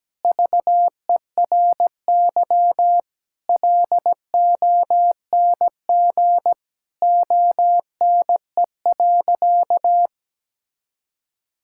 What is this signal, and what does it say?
Morse code, heard as a keyed tone, about 17 wpm, 700 Hz. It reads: VERY LONG ONE.
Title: Sentences from Top 1000 Words - Review Part 5 - 17wpm